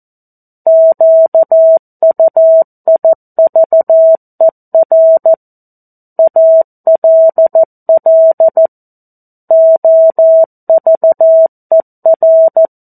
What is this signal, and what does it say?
Morse code, heard as a keyed tone, about 14 wpm, 650 Hz. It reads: QUIVER ALL OVER
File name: 02-Chapter-02 14 WPM